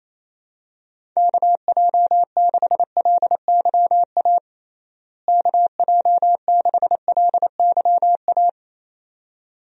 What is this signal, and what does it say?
Morse code, heard as a keyed tone, about 28 wpm, 700 Hz. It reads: KJ6LYA KJ6LYA
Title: USA Call Signs - 6 Characters - Repeated 2x - 28wpm